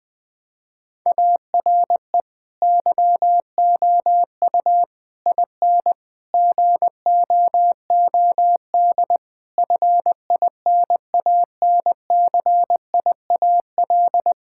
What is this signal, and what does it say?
Morse code, heard as a keyed tone, about 20 wpm, 700 Hz. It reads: ARE YOU IN GOOD FINANCIAL
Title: Sentences from Top 1000 Words - Review Part 5 - 20wpm